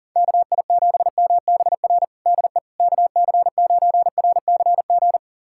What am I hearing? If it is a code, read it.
Morse code, heard as a keyed tone, about 40 wpm, 700 Hz. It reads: KI7MBR DE KC9RCG
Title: USA Callsign DE Callsign - 6 Characters - 40wpm